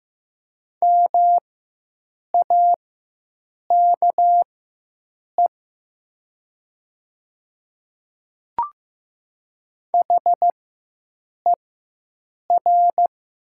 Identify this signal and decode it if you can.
Morse code, heard as a keyed tone, about 15 wpm, spaced as for 7 wpm, 700 Hz. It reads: MAKE HER
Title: 4x Character Spacing - Top 100 Words - 15wpm